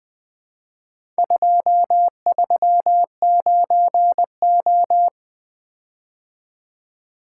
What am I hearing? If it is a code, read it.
Morse code, heard as a keyed tone, about 22 wpm, 700 Hz. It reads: 239O
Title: Speed-Racing - Four Letter-Number Combinations - 20wpm